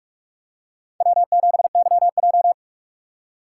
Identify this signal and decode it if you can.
Morse code, heard as a keyed tone, about 44 wpm, 700 Hz. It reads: W7YJ